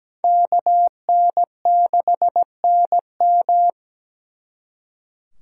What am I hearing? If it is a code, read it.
Morse code, heard as a keyed tone, about 20 wpm, 700 Hz. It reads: KN6NM
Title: Speed Racing - USA 2x2 Call Signs 17wpm